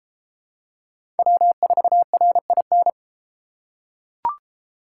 Morse code, heard as a keyed tone, about 33 wpm, 700 Hz. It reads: W4RIN